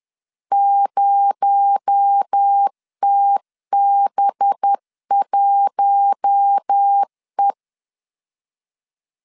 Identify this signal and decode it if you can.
Morse code, heard as a keyed tone, about 10 wpm, 800 Hz. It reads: 0TB1E
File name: LEZIONI / 7 GF9Z / 7 GF9 10wpm 2747